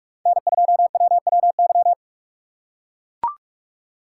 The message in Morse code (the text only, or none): N2WWY